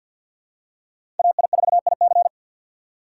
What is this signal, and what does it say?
Morse code, heard as a keyed tone, about 47 wpm, 700 Hz. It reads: AI4IC
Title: USA Call Signs - 5 Characters - 3x Character Spacing - 50wpm